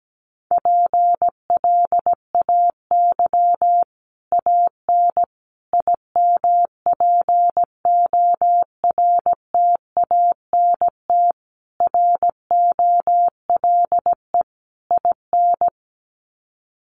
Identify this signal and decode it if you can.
Morse code, heard as a keyed tone, about 17 wpm, 700 Hz. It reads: PLAY AN IMPORTANT ROLE IN
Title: Sets of 5 Words Encore 17wpm